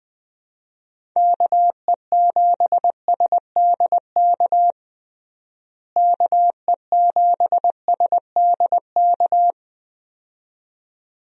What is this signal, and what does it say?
Morse code, heard as a keyed tone, about 20 wpm, 700 Hz. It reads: KE7SDK KE7SDK
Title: USA Call Signs - 6 Characters - Repeated 2x - 20wpm